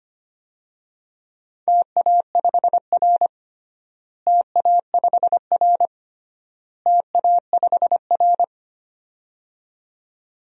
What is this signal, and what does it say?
Morse code, heard as a keyed tone, about 25 wpm, 700 Hz. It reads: TA5R TA5R TA5R
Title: DXCC-100 Call Signs - 4 Characters - Repeated 3x - 25wpm